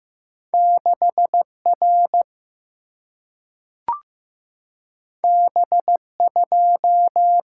Morse code, heard as a keyed tone, about 15 wpm, 700 Hz. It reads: 6R B2
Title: Single-Letter Single-Number 15wpm